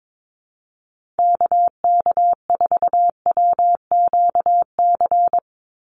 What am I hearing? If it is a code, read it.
Morse code, heard as a keyed tone, about 22 wpm, 700 Hz. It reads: KK4WQC
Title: USA Call Signs 22wpm